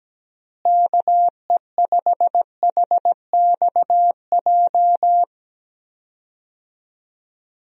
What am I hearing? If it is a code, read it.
Morse code, heard as a keyed tone, about 17 wpm, 700 Hz. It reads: KE5HXJ
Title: USA Call Signs - 6 Characters - Repeated 2x - 17wpm